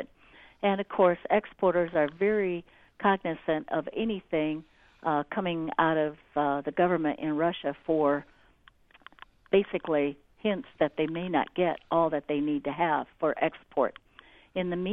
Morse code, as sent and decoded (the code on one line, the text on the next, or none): none